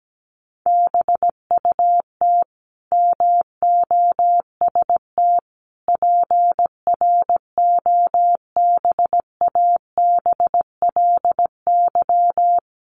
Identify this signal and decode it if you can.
Morse code, heard as a keyed tone, about 17 wpm, 700 Hz. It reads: BUT MOST PROBABLY
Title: Sentences from Top 1500 Words 17wpm